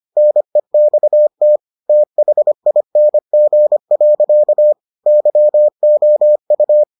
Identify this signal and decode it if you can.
Morse code, heard as a keyed tone, about 25 wpm, 600 Hz. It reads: NEXT THING. YOU